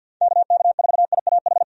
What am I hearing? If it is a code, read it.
Morse code, heard as a keyed tone, about 49 wpm, 700 Hz. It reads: KK4IAH
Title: USA Call Signs - 6 Characters - 50wpm